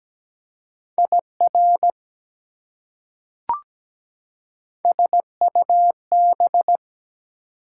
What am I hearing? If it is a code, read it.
Morse code, heard as a keyed tone, about 17 wpm, 700 Hz. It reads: IR SUB
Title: Prefixes and Suffixes 17wpm